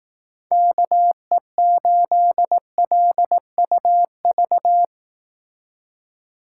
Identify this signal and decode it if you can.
Morse code, heard as a keyed tone, about 18 wpm, 700 Hz. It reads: KE8LUV
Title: Speed Racing - USA 2x3 Call Signs 15wpm